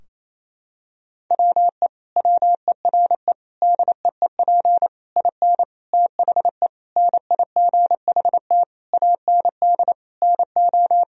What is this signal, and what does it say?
Morse code, heard as a keyed tone, about 27 wpm, 700 Hz. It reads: WE WERE DEEP IN THE NIGHT AND NO
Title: Sentences from Top 1000 Words - Review Part 3 - 28wpm